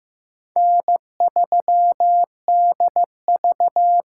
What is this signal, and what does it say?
Morse code, heard as a keyed tone, about 15 wpm, 700 Hz. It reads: N3DV
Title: USA Call Signs - 5 Characters - 8x Character Spacing - 15wpm